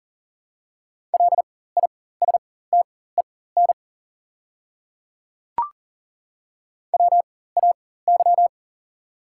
Morse code, heard as a keyed tone, about 40 wpm, spaced as for 19 wpm, 690 Hz. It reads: LISTEN WAY
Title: 4x Character Spacing - Top 500 Words - 40wpm